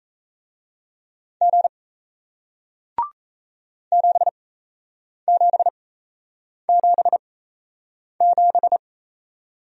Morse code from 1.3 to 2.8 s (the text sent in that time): G